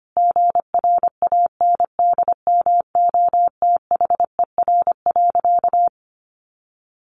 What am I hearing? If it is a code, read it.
Morse code, heard as a keyed tone, about 25 wpm, 700 Hz. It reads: GRANDMOTHER.
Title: Sentences from Top 1200 Words 25wpm